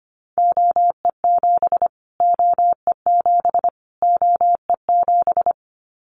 Morse code, heard as a keyed tone, about 25 wpm, 700 Hz. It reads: OE7 OE7 OE7
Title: Callsign Prefixes for Top 25 DXCC - Repeated 3x - 25wpm